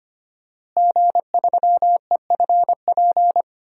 Morse code, heard as a keyed tone, about 25 wpm, 700 Hz. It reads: G3EFP
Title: DXCC-100 Call Signs - 5 Characters - Repeated 3x - 25wpm